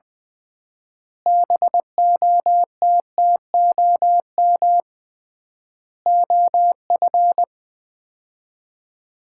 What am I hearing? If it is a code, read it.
Morse code, heard as a keyed tone, about 20 wpm, 700 Hz. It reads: BOTTOM OF
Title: 3x Word-Spacing - Sets of 3 Words - 20wpm